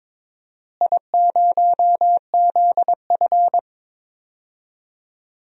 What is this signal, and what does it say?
Morse code, heard as a keyed tone, about 21 wpm, 700 Hz. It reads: I0ZF
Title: DXCC-100 Call Signs - 4 Characters - Repeated 3x - 22wpm